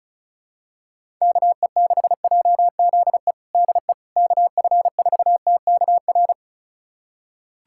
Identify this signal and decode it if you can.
Morse code, heard as a keyed tone, about 36 wpm, 700 Hz. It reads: KE6JZE DE KF4TKR
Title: USA Callsign DE Callsign - 6 Characters - 35wpm